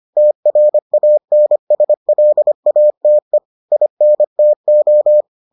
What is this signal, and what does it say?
Morse code, heard as a keyed tone, about 25 wpm, 600 Hz. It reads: TRANSLATE INTO